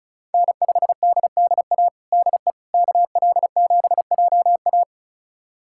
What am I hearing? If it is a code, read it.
Morse code, heard as a keyed tone, about 36 wpm, 700 Hz. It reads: N5DDA DE KL7JA
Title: USA Callsign DE Callsign - 5 Characters - 35wpm